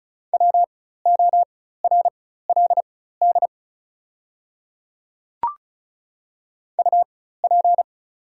Morse code, heard as a keyed tone, about 36 wpm, spaced as for 16 wpm, 700 Hz. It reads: WORLD UP
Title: 4x Character Spacing - Top 200 Words - 35wpm